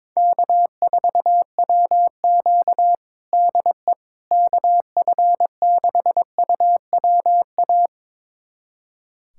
Morse code, heard as a keyed tone, about 22 wpm, 700 Hz. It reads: K4WQ DE KF6UWA
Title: USA Callsign DE Callsign 22wpm